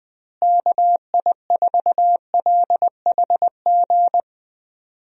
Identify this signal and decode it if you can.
Morse code, heard as a keyed tone, about 20 wpm, 700 Hz. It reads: KI4LHG
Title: USA Call Signs - 6 Characters - Repeated 2x - 20wpm